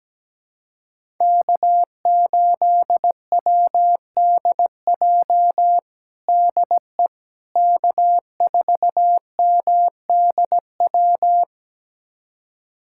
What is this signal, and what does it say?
Morse code, heard as a keyed tone, about 17 wpm, 700 Hz. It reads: K8WDJ DE K4MDW
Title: USA Callsign DE Callsign - 5 Characters - 17wpm